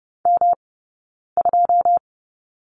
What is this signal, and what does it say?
Morse code, heard as a keyed tone, about 30 wpm, spaced as for 9 wpm, 690 Hz. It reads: M2
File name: Lesson 18d - Letter m - Two Characters 30wpm